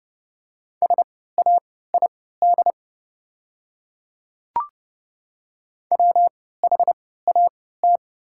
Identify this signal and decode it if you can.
Morse code, heard as a keyed tone, about 30 wpm, spaced as for 17 wpm, 700 Hz. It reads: SAID WHAT
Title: 3x Character Spacing - Top 100 Words - 30wpm